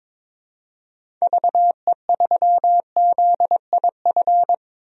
Morse code, heard as a keyed tone, about 22 wpm, 700 Hz. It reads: VE3ZIF
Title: DXCC-100 Call Signs - 6 Characters - 22wpm